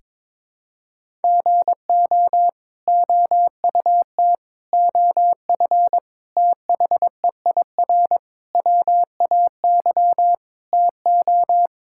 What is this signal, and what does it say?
Morse code, heard as a keyed tone, about 22 wpm, 700 Hz. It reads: GO OUT OF THEIR WAY TO
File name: practice / 22wpm / Sets of 6 Words - 22wpm